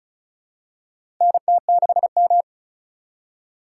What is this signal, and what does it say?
Morse code, heard as a keyed tone, about 34 wpm, 700 Hz. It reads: NT6M
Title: USA Call Signs - 4 Characters - 4x Character Spacing - 35wpm